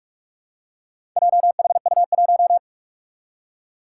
Morse code, heard as a keyed tone, about 43 wpm, 700 Hz. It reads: JHU1
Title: Four Challenging Letter-Number Combinations 45wpm